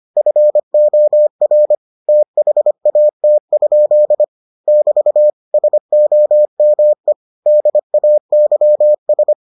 Morse code, heard as a keyed tone, about 25 wpm, 600 Hz. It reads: FOR THAT? <BT> SOME DAYS